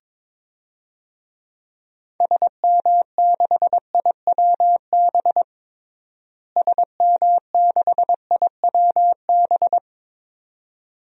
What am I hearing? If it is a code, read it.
Morse code, heard as a keyed tone, about 22 wpm, 700 Hz. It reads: SM6IWB SM6IWB
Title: DXCC-100 Call Signs - 6 Characters - Repeated 2x - 22wpm